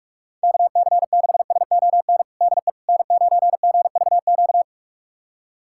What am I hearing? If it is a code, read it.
Morse code, heard as a keyed tone, about 46 wpm, 700 Hz. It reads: KC6SON DE N9GVQ
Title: USA Callsign DE Callsign 45wpm